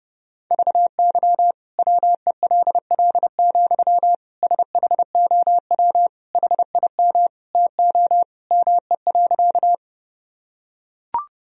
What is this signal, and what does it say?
Morse code, heard as a keyed tone, about 30 wpm, 700 Hz. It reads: VY WELL, SHOW HIM TO ME.
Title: Sentences from Top 200 Words and Abbreviations 30wpm